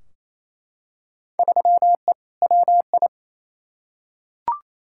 Morse code, heard as a keyed tone, about 28 wpm, 700 Hz. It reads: 3E WI
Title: Field Day Exchanges Encore - 28wpm